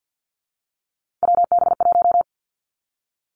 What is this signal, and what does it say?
Morse code, heard as a keyed tone, about 50 wpm, 700 Hz. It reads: WB1
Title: Three Challenging Letter-Number Combinations 50wpm